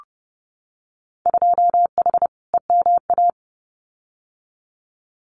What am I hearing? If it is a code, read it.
Morse code, heard as a keyed tone, about 30 wpm, 700 Hz. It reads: 2H EMA
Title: Winter Field Day Exchanges 30wpm